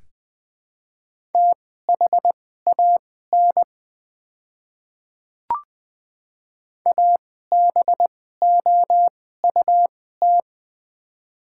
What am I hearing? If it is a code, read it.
Morse code, heard as a keyed tone, about 20 wpm, spaced as for 14 wpm, 700 Hz. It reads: THAN ABOUT
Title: 2x Character Spacing - Top 100 Words - 20wpm